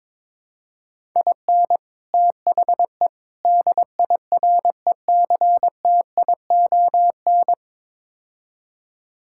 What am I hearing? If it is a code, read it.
Morse code, heard as a keyed tone, about 22 wpm, 700 Hz. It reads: IN THE DIRECTION
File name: practice / 22wpm / Build A Phrase - 5-Word Phrases - 22wpm